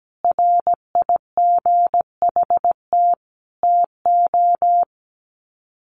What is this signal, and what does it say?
Morse code, heard as a keyed tone, about 17 wpm, 700 Hz. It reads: RIGHT TO
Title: Sets of 4 Words 17wpm